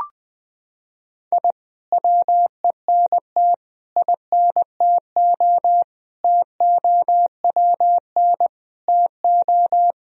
Morse code, heard as a keyed tone, about 20 wpm, 700 Hz. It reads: I WENT INTO TOWN TO
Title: Sentences from Top 1000 Words - Review Part 4 - 20wpm